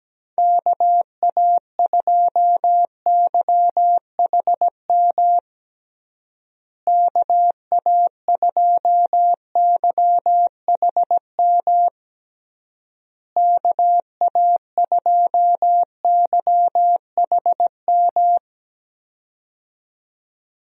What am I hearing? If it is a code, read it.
Morse code, heard as a keyed tone, about 17 wpm, 700 Hz. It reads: KA2YHM KA2YHM KA2YHM
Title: USA Call Signs - 6 Characters - Repeated 3x - 17wpm